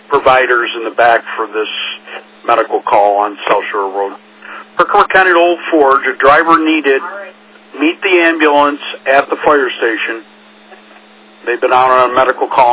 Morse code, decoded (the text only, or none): none